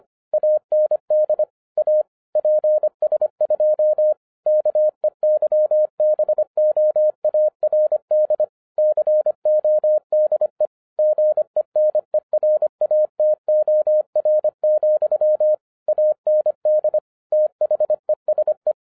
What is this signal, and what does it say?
Morse code, heard as a keyed tone, about 25 wpm, 610 Hz. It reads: AND A PS2 KEYBOARD CODE GENERATOR, AND THESE
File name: 2011-10-mp3 / n09-CW TO PS2 CONVERTER 25wpm